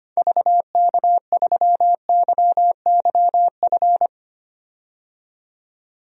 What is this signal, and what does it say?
Morse code, heard as a keyed tone, about 25 wpm, 700 Hz. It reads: VK3YYF
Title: DXCC-100 Call Signs - 6 Characters - 25wpm